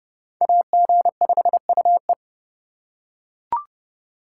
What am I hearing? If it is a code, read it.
Morse code, heard as a keyed tone, about 29 wpm, 700 Hz. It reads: AG5UE